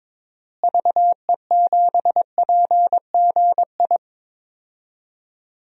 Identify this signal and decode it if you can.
Morse code, heard as a keyed tone, about 21 wpm, 700 Hz. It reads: VE7PGI
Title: DXCC-100 Call Signs - 6 Characters - Repeated 2x - 22wpm